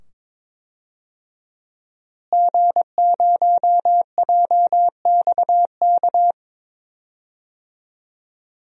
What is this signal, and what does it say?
Morse code, heard as a keyed tone, about 22 wpm, 700 Hz. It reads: G0JXK
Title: DXCC-100 Call Signs - 5 Characters - Repeated 2x - 22wpm